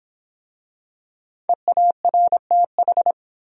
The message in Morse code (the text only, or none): EARTH